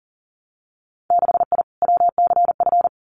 Morse code, heard as a keyed tone, about 38 wpm, 700 Hz. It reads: 6I WCF
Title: Winter Field Day Exchanges Encore 40wpm